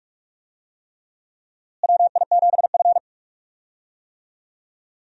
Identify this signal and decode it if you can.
Morse code, heard as a keyed tone, about 41 wpm, 700 Hz. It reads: WI7F
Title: USA Call Signs - 4 Characters - Repeated 3x - 45wpm